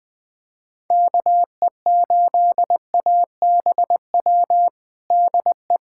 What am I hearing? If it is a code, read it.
Morse code, heard as a keyed tone, about 19 wpm, 700 Hz. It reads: KE8ABW DE
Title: USA Callsign DE Callsign - 6 Characters - 20wpm